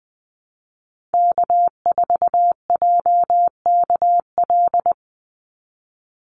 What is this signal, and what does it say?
Morse code, heard as a keyed tone, about 20 wpm, 700 Hz. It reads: K4JKL
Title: Call Signs Repeated 3x - 20wpm